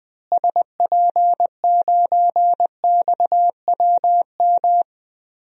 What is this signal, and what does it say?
Morse code, heard as a keyed tone, about 20 wpm, 700 Hz. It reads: SP9XWM
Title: DXCC-100 Call Signs - 6 Characters - Repeated 3x - 20wpm